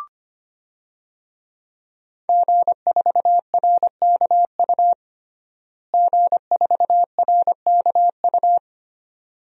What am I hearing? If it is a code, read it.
Morse code, heard as a keyed tone, about 25 wpm, 700 Hz. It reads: G4RKU G4RKU
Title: DXCC-100 Call Signs - 5 Characters - Repeated 2x - 25wpm